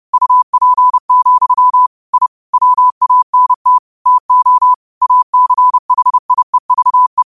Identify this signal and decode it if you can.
Morse code, heard as a keyed tone, about 30 wpm, 1000 Hz. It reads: UP, I WANT TO ACHIEVE